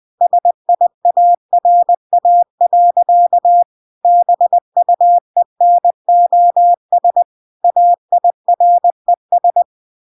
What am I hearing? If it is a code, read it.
Morse code, heard as a keyed tone, about 20 wpm, 700 Hz. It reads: SIARA. BUENOS AIRES
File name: recordings / 20wpm0000